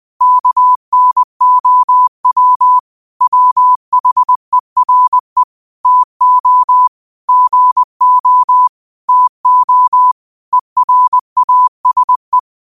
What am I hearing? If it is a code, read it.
Morse code, heard as a keyed tone, about 20 wpm, 1000 Hz. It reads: KNOW WHERE TO GO TO ERASE